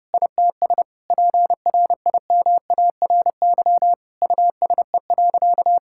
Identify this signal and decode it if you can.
Morse code, heard as a keyed tone, about 32 wpm, 700 Hz. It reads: ITS PRIMARY USE.
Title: Sentences from Top 1000 Words - Review Part 5 - 30wpm